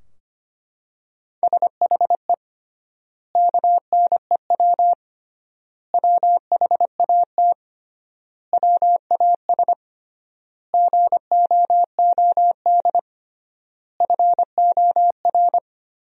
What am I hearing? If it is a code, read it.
Morse code, heard as a keyed tone, about 25 wpm, 700 Hz. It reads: SHE KNEW WHAT WAS GOOD FOR
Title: 3x Word-Spacing - Sentences from Top 400 Words - 25wpm